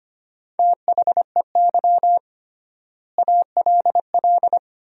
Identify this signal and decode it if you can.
Morse code, heard as a keyed tone, about 25 wpm, 700 Hz. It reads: THEY ALL
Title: 3x Word-Spacing - Sentences from Top 500 Words - 25wpm